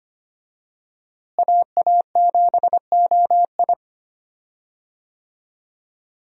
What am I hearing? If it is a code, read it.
Morse code, heard as a keyed tone, about 25 wpm, 700 Hz. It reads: AA7OI
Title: USA Call Signs - 5 Characters - Repeated 3x - 25wpm